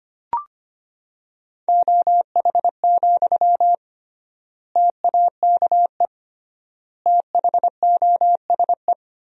Morse code, heard as a keyed tone, about 25 wpm, 700 Hz. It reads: OH, TAKE THOSE
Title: 3x Word-Spacing - Sentences from Top 500 Words - 25wpm